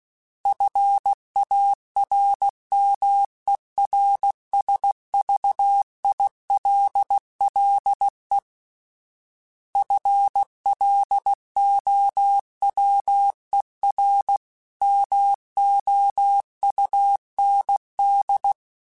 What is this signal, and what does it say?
Morse code, heard as a keyed tone, about 16 wpm, 800 Hz. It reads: FARMERSVILLE FLOWER MOUND
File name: fromSMs / FromSMs mp3s / Cities@15